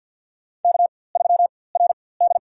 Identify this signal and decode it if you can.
Morse code, heard as a keyed tone, about 46 wpm, spaced as for 23 wpm, 700 Hz. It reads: K3RD
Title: USA Call Signs - 4x Character Spacing - 50wpm